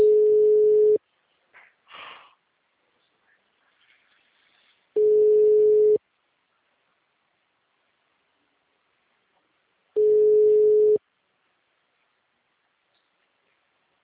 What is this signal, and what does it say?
Ringback tone (Europe, 425 Hz).
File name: out- -2012- - - .8